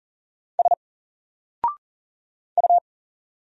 Morse code, heard as a keyed tone, about 40 wpm, 700 Hz. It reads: S U
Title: Single Letter-Number Alt 40wpm